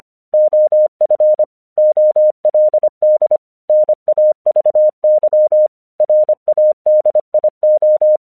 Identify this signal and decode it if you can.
Morse code, heard as a keyed tone, about 26 wpm, 610 Hz. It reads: OF OLD NAVY RADIO
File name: 1 English 25wpm / AQC13